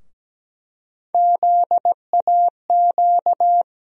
Morse code, heard as a keyed tone, about 17 wpm, 700 Hz. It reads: ZAQ
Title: USA Call Sign Postfixes - 3 Letters 17wpm